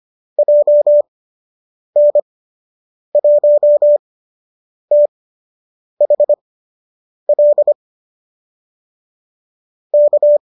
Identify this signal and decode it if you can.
Morse code, heard as a keyed tone, about 25 wpm, spaced as for 8 wpm, 590 Hz. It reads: JN1THL K
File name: qso003 08